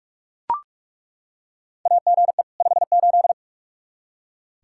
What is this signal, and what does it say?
Morse code, heard as a keyed tone, about 41 wpm, 700 Hz. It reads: AGE 58
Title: Practice QSO - Exchange 3 - More Info - 45wpm